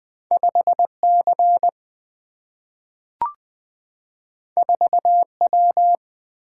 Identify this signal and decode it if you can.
Morse code, heard as a keyed tone, about 20 wpm, 700 Hz. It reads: 5 C 4 W
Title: Single-Letter Single-Number 20wpm